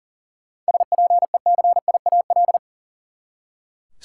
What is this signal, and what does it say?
Morse code, heard as a keyed tone, about 40 wpm, 700 Hz. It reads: SPECIAL
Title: Seven Letter Words - 40wpm